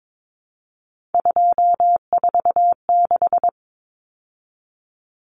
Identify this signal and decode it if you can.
Morse code, heard as a keyed tone, about 22 wpm, 700 Hz. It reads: 246
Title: Numbers - 3-Digit 22wpm